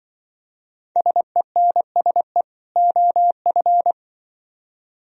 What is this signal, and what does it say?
Morse code, heard as a keyed tone, about 24 wpm, 700 Hz. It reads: SENSE OF
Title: Speed-Racing - Sets of 2 Words 20wpm